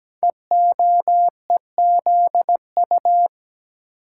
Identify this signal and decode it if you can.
Morse code, heard as a keyed tone, about 17 wpm, 700 Hz. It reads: EOEZU
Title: Code Groups - 5 Letters - 17wpm